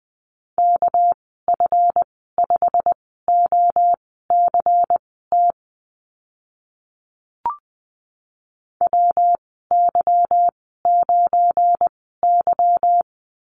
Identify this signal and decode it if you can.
Morse code, heard as a keyed tone, about 20 wpm, spaced as for 14 wpm, 700 Hz. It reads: KF5OCT WY9Y
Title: 231 Lesson 37g - Letter x - Call Signs 20wpm